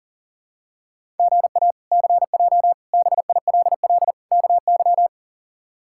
Morse code, heard as a keyed tone, about 40 wpm, 700 Hz. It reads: GA CJ BILL KY
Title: K1USN SST QSO Practice - 40wpm